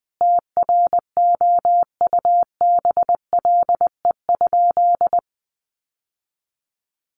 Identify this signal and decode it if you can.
Morse code, heard as a keyed tone, about 20 wpm, 700 Hz. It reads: TROUBLE?
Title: Sentences from Top 1200 Words 20wpm